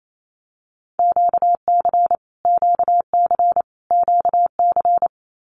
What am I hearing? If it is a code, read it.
Morse code, heard as a keyed tone, about 28 wpm, 700 Hz. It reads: QC QC QC
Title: Canadian Provinces and Territories - Repeated 3x - 28wpm